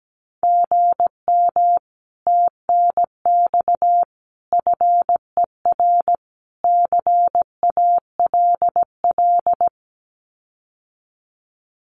Morse code, heard as a keyed tone, about 17 wpm, 700 Hz. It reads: GM TNX FER CALL
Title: Common QSO Phrases Encore 17wpm